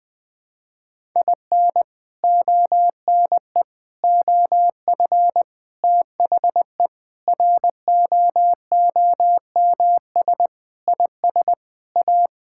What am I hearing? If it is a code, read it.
Morse code, heard as a keyed tone, about 20 wpm, 700 Hz. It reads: IN ONE OF THE ROOMS IS A
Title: Sentences with 4-Word Phrases - 20wpm